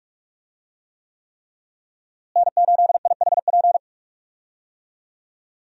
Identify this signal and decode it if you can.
Morse code, heard as a keyed tone, about 46 wpm, 700 Hz. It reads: N8IHP
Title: USA Call Signs - 5 Characters - Repeated 3x - 45wpm